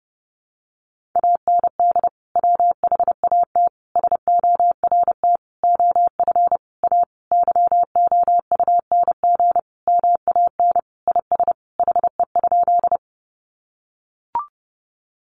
Morse code, heard as a keyed tone, about 30 wpm, 700 Hz. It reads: AND WHAT SORT OF A YOUNG MAN IS HE?
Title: Sentences from Top 1300 Words 30wpm